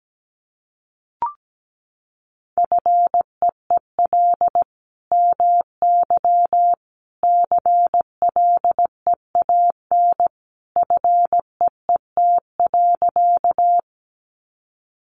Morse code, heard as a keyed tone, about 17 wpm, 700 Hz. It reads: FEEL MY CLEAN FEET.
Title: Sentences from Top 1100 Words 17wpm